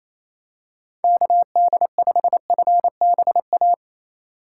Morse code, heard as a keyed tone, about 28 wpm, 700 Hz. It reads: KD5FBA
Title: K1USN SST QSO Practice - 28wpm